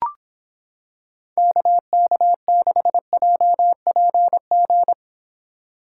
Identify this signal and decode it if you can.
Morse code, heard as a keyed tone, about 26 wpm, 700 Hz. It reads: KK6JPG